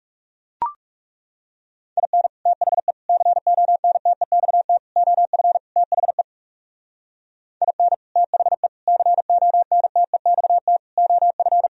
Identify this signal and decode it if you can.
Morse code, heard as a keyed tone, about 37 wpm, 700 Hz. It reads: IN THE CONTEXT OF THE IN THE CONTEXT OF
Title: Speed-Racing - Sets of 5 Words 30wpm